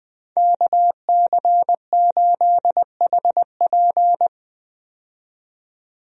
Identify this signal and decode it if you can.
Morse code, heard as a keyed tone, about 20 wpm, 700 Hz. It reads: KC8HP
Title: USA Call Signs - 5 Characters - 20wpm